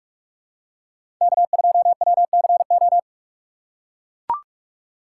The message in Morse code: K2WCO

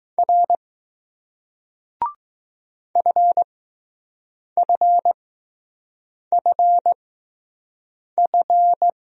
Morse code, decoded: R FFFF

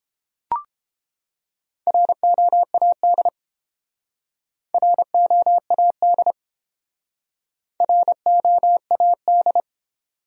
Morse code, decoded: ROAD ROAD ROAD

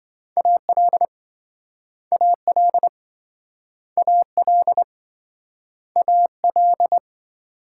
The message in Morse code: AL AL AL AL

